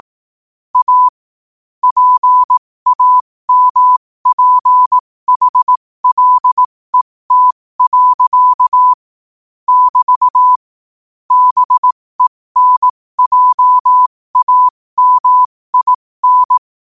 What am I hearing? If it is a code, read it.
Morse code, heard as a keyed tone, about 18 wpm, spaced as for 15 wpm, 1000 Hz. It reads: A PAMPHLET. <BT> BENJAMIN